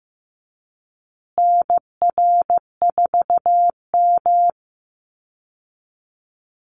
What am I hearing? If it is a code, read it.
Morse code, heard as a keyed tone, about 15 wpm, 700 Hz. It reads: NR4M
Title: Call Signs Repeated 2x - 15wpm